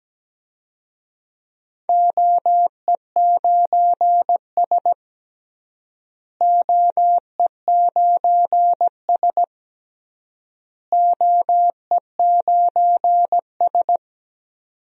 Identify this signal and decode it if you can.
Morse code, heard as a keyed tone, about 17 wpm, 700 Hz. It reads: OE9S OE9S OE9S
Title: DXCC-100 Call Signs - 4 Characters - Repeated 3x - 17wpm